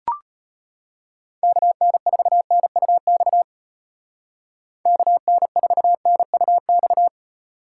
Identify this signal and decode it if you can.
Morse code, heard as a keyed tone, about 34 wpm, 700 Hz. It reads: KN4NUX KN4NUX